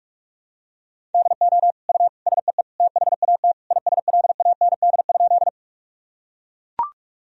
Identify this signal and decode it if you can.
Morse code, heard as a keyed tone, about 41 wpm, 700 Hz. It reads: DO U SEE THAT ISLAND?
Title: Sentences from Top 500 Words and Abbreviations 45wpm